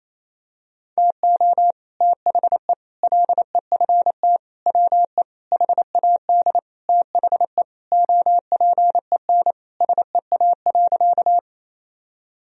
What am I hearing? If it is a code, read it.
Morse code, heard as a keyed tone, about 28 wpm, 700 Hz. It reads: TO THE LEFT WE HAD THE OPEN SEA.
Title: Sentences from Top 1000 Words - Review Part 1 - 28wpm